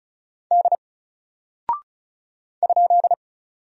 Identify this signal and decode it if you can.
Morse code, heard as a keyed tone, about 36 wpm, 700 Hz. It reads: D ?
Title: Single Character (Letter:Number:Punctuation:Prosign) Alt 35wpm